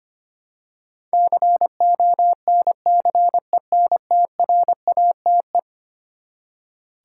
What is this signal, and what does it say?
Morse code, heard as a keyed tone, about 25 wpm, 700 Hz. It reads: CONCENTRATE
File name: practice / 25wpm / Eleven Letter Words - 25wpm